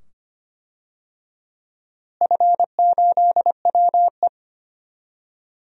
Morse code, heard as a keyed tone, about 25 wpm, 700 Hz. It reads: F8WE
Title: DXCC-100 Call Signs - 4 Characters - Repeated 2x - 25wpm